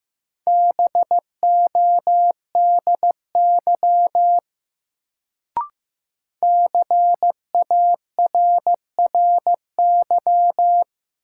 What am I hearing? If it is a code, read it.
Morse code, heard as a keyed tone, about 15 wpm, 700 Hz. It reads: BODY CARRY
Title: Sentences from Top 400 Words - New Word Drill - 15wpm